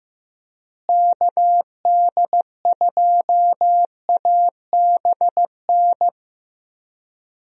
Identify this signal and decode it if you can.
Morse code, heard as a keyed tone, about 15 wpm, 700 Hz. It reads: KD2ABN
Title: USA Call Signs - 6 Characters - Repeated 2x - 15wpm